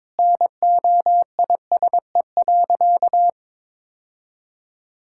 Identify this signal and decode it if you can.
Morse code, heard as a keyed tone, about 22 wpm, 700 Hz. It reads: NOISE.
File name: practice / 22wpm / Sentences with 2-Word Phrases - 22wpm